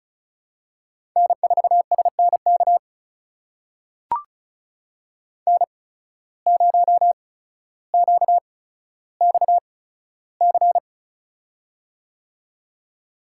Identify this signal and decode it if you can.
Morse code, heard as a keyed tone, about 36 wpm, spaced as for 15 wpm, 700 Hz. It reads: N4SNK N0QXC
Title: USA Call Signs - 5 Characters - 8x Character Spacing - 35wpm